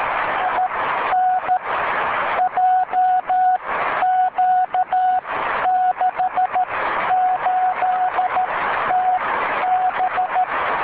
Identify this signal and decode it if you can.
Morse code, heard as a keyed tone, about 13 wpm, spaced as for 11 wpm, 740 Hz, with noise. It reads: IN JQ68TB